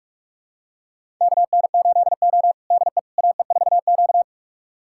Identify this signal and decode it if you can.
Morse code, heard as a keyed tone, about 44 wpm, 700 Hz. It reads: KN8O DE AE4Q
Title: USA Callsign DE Callsign - 4 Characters - 45wpm